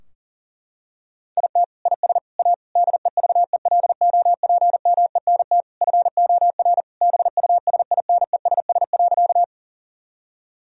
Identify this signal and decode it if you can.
Morse code, heard as a keyed tone, about 40 wpm, 700 Hz. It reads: IT IS A DEVELOPMENT FOR BUSINESS.
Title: Sentences from Top 1000 Words - Review Part 4 - 40wpm